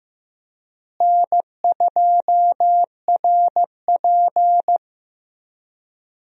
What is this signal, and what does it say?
Morse code, heard as a keyed tone, about 16 wpm, 700 Hz. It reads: N2RP
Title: USA Call Signs - 4 Characters - Repeated 3x - 15wpm